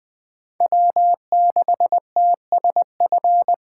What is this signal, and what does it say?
Morse code, heard as a keyed tone, about 21 wpm, 700 Hz. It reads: W6TSF
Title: Speed Racing - USA Call Signs 17wpm